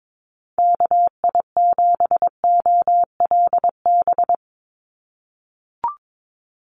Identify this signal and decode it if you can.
Morse code, heard as a keyed tone, about 22 wpm, 700 Hz. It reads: KI7OLB
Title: USA Call Signs 22wpm